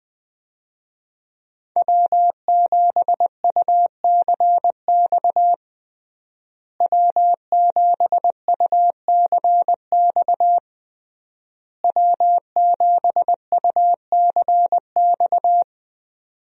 Morse code, heard as a keyed tone, about 20 wpm, 700 Hz. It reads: W7UCX W7UCX W7UCX
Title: USA Call Signs - 5 Characters - Repeated 3x - 20wpm